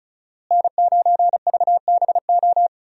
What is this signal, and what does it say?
Morse code, heard as a keyed tone, about 34 wpm, 700 Hz. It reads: N9VBO
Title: USA Call Signs - 5 Characters - Repeated 3x - 35wpm